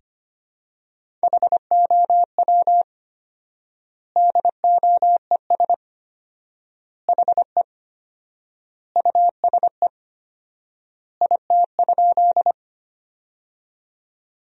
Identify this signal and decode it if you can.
Morse code, heard as a keyed tone, about 25 wpm, 700 Hz. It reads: HOW DOES HE USE IT?
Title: 4x Word-Spacing - Sentences from Top 200 Words - 25wpm